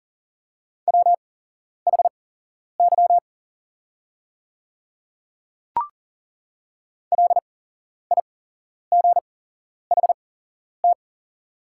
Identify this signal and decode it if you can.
Morse code, heard as a keyed tone, about 40 wpm, spaced as for 11 wpm, 700 Hz. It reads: WHY LIGHT
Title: 8x Character Spacing - Top 200 Words - 40wpm